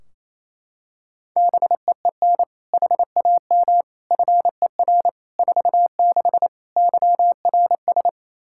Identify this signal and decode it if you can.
Morse code, heard as a keyed tone, about 28 wpm, 700 Hz. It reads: BEEN HAM FER 46 YRS
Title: K1USN SST QSO Practice 28wpm